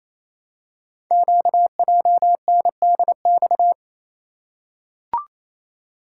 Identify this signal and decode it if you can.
Morse code, heard as a keyed tone, about 27 wpm, 700 Hz. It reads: QJNDX
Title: Code Groups - 5 Letters - 28wpm